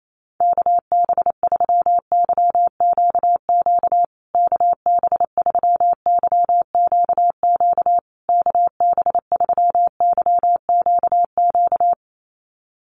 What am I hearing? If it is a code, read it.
Morse code, heard as a keyed tone, about 28 wpm, 700 Hz. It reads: KB3YQQ KB3YQQ KB3YQQ
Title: USA Call Signs Repeated 3x 28wpm